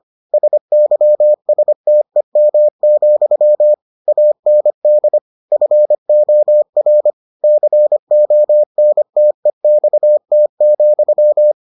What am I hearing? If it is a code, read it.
Morse code, heard as a keyed tone, about 25 wpm, 600 Hz. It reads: SYSTEM, AND FOR CONTEXT,